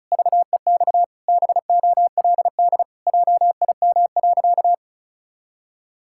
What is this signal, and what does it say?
Morse code, heard as a keyed tone, about 34 wpm, 700 Hz. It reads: VEX BOLD JIM.
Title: Pangram Challenge v2 - 35wpm